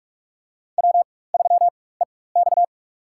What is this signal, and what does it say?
Morse code, heard as a keyed tone, about 44 wpm, spaced as for 21 wpm, 700 Hz. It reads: W3EX
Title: USA Call Signs - 4 Characters - 4x Character Spacing - 45wpm